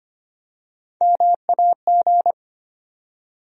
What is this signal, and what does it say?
Morse code, heard as a keyed tone, about 25 wpm, 700 Hz. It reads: MAG
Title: USA Call Sign Postfixes - 3 Letters 25wpm